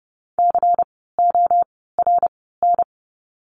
Morse code, heard as a keyed tone, about 30 wpm, spaced as for 17 wpm, 700 Hz. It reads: CORN
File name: Lesson 16f - Letter c - Words 30wpm